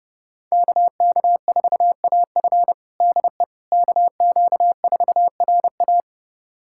Morse code, heard as a keyed tone, about 32 wpm, 700 Hz. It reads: KK4AF DE KQ4RA
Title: USA Callsign DE Callsign - 5 Characters - 30wpm